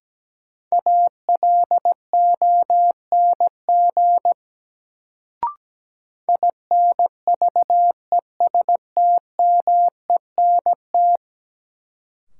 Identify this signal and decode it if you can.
Morse code, heard as a keyed tone, about 17 wpm, 700 Hz. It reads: ALONG INVESTMENT
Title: Sentences from Top 700 Words - New Word Drill - 17wpm